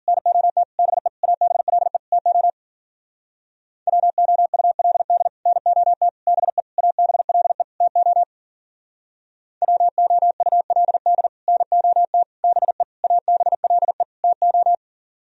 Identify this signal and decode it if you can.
Morse code, heard as a keyed tone, about 49 wpm, 700 Hz. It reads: NOT BE ABLE TO WOULD NOT BE ABLE TO WOULD NOT BE ABLE TO
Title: Speed-Racing - Sets of 5 Words 40wpm